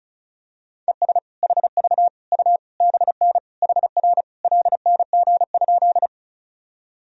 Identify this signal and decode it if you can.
Morse code, heard as a keyed tone, about 36 wpm, 700 Hz. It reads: ES HV U BN HR LNG?
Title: Sentences from Top 200 Words and Abbreviations 35wpm